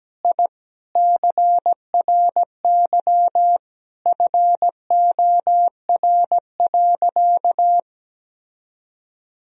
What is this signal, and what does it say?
Morse code, heard as a keyed tone, about 17 wpm, 700 Hz. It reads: I CRY FOR.
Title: Sentences from Top 1000 Words - Review Part 2 - 17wpm